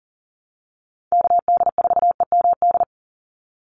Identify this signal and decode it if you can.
Morse code, heard as a keyed tone, about 40 wpm, 700 Hz. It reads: KD4EMD
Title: USA Call Signs 40wpm